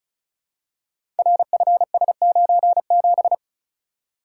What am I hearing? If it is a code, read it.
Morse code, heard as a keyed tone, about 36 wpm, 700 Hz. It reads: RFS97
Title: Code Groups - 5 Letter:Numbers - 35wpm